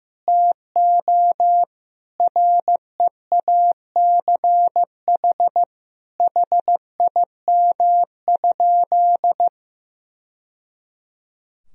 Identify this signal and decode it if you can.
Morse code, heard as a keyed tone, about 15 wpm, 700 Hz. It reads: TO REACH HIM?
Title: Sentences from Top 1000 Words - Review Part 2 - 15wpm